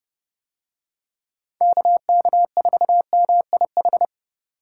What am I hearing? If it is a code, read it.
Morse code, heard as a keyed tone, about 29 wpm, 700 Hz. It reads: KK4MIH